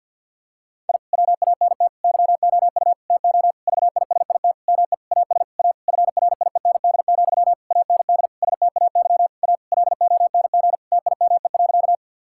Morse code, heard as a keyed tone, about 50 wpm, 700 Hz. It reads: I WANT YOU TO VISIT ME AS A FRIEND, AND STAY A LONG TIME.
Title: Sentences from Top 1000 Words - Review Part 6 - 50wpm